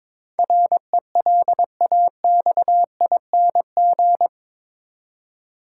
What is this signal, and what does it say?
Morse code, heard as a keyed tone, about 21 wpm, 700 Hz. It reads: RELAXING
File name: practice / 22wpm / Eight Letter Words - 22wpm